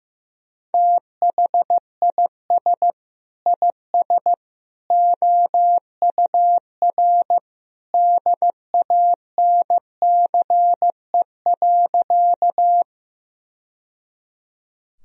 Morse code, heard as a keyed tone, about 15 wpm, 700 Hz. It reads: THIS IS OUR DANCE.
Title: Sentences from Top 1000 Words - Review Part 4 - 15wpm